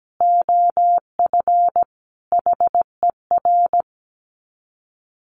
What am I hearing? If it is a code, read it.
Morse code, heard as a keyed tone, about 17 wpm, 700 Hz. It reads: OF HER
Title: Sets of 4 Words 17wpm